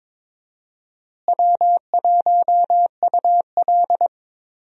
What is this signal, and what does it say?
Morse code, heard as a keyed tone, about 22 wpm, 700 Hz. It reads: W1UL
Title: USA Call Signs - 4 Characters - Repeated 2x - 22wpm